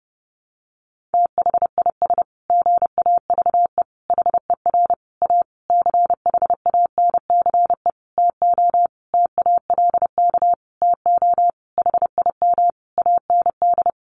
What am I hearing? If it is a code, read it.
Morse code, heard as a keyed tone, about 30 wpm, 700 Hz. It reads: THIS GAVE HER A CHANCE TO TALK TO HIM AND
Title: Sentences from Top 1400 Words 30wpm